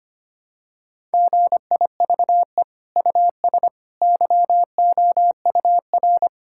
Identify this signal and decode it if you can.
Morse code, heard as a keyed tone, about 25 wpm, 700 Hz. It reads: GIVE US YOUR
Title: Sentences from Top 1000 Words - Review Part 5 - 25wpm